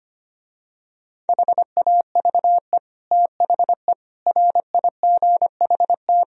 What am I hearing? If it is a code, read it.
Morse code, heard as a keyed tone, about 24 wpm, 700 Hz. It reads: HAVE THE RIGHT
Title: Speed-Racing - Sets of 4 Words 25wpm